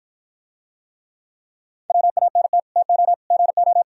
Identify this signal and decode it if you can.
Morse code, heard as a keyed tone, about 54 wpm, 700 Hz. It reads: WANT TO GO